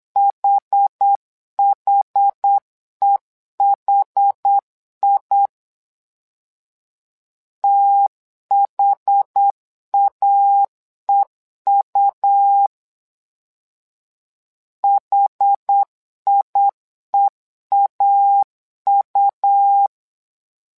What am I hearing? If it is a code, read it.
Morse code, heard as a keyed tone, about 8 wpm, 800 Hz. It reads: HHEHI THAEU HIEAU
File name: LEZIONI / 2 UHDN / 2 UH 8wpm 2129